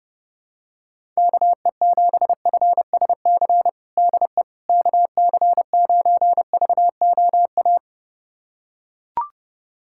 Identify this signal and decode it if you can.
Morse code, heard as a keyed tone, about 30 wpm, 700 Hz. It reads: KE7FSC DE KC9VOA
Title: USA Callsign DE Callsign - 6 Characters - 30wpm